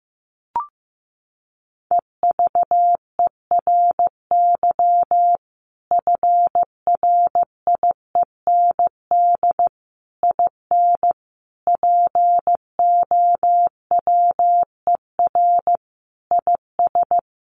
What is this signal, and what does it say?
Morse code, heard as a keyed tone, about 15 wpm, 700 Hz. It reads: EVERY FRIEND IN POWER IS
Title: Sentences from Top 1400 Words 15wpm